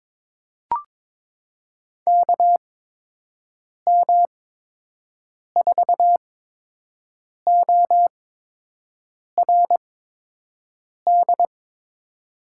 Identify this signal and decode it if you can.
Morse code, heard as a keyed tone, about 22 wpm, spaced as for 6 wpm, 700 Hz. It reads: KM4ORD
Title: USA Call Signs - 8x Character Spacing - 22wpm